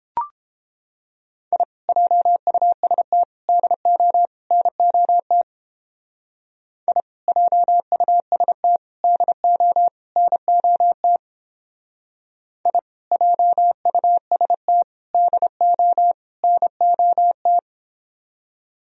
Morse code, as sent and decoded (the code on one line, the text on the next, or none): .. / .--- ..- ... - / -.. --- / -. --- - / .. / .--- ..- ... - / -.. --- / -. --- - / .. / .--- ..- ... - / -.. --- / -. --- -
I JUST DO NOT I JUST DO NOT I JUST DO NOT